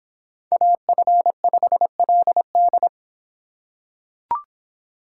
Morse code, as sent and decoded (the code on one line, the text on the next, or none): .- ..-. ..... .-.. -..
AF5LD